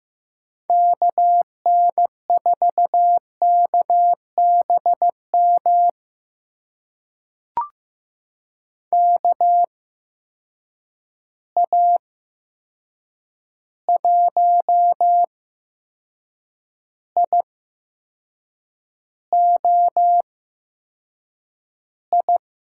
Morse code, heard as a keyed tone, about 15 wpm, spaced as for 6 wpm, 700 Hz. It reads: KN4KBM KA1IOI
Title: USA Call Signs - 6 Characters - 8x Character Spacing - 15wpm